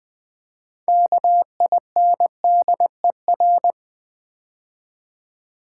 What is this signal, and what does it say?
Morse code, heard as a keyed tone, about 20 wpm, 700 Hz. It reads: KINDER
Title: Build A Word - 6 Letters At A Time - 12-Letter Words - 20wpm